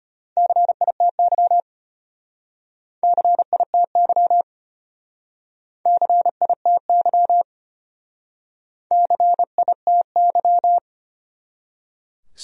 Morse code, heard as a keyed tone, about 36 wpm, 700 Hz. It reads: CITY CITY CITY CITY